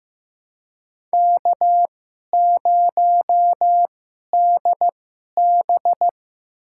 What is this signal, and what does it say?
Morse code, heard as a keyed tone, about 16 wpm, spaced as for 11 wpm, 700 Hz. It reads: K0DB
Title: USA Call Signs - 2x Character Spacing - 15wpm